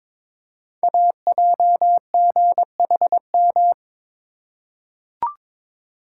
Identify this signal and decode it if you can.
Morse code, heard as a keyed tone, about 23 wpm, 700 Hz. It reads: AJGHM
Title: Code Groups - 5 Letters - 22wpm